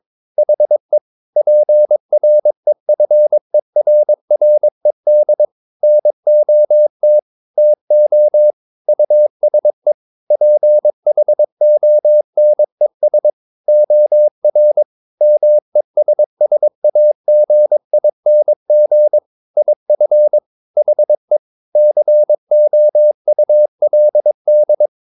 Morse code, heard as a keyed tone, about 22 wpm, 600 Hz. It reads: HE PREFERRED NOT TO USE PHONES OR MESSAGING IF HE COULD
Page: Romanen Never af Ken Follet på engelsk.